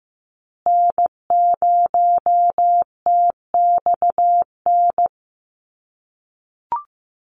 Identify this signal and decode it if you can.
Morse code, heard as a keyed tone, about 15 wpm, 700 Hz. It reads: N0TXN
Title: USA Call Signs Repeated 2x 15wpm